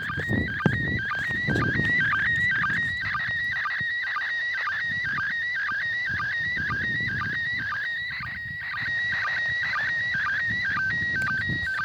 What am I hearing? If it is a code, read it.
SSTV (slow-scan television), PD 120.
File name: 08-ARISS-SSTV-Expedition-63-ARISS-Series-16-Satellites